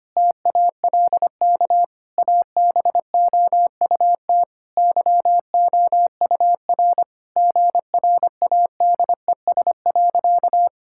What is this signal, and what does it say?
Morse code, heard as a keyed tone, about 25 wpm, 700 Hz. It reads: TALK ABOUT YOUR GRADES.
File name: practice / 25wpm / Sentences with 3-Word Phrases - 25wpm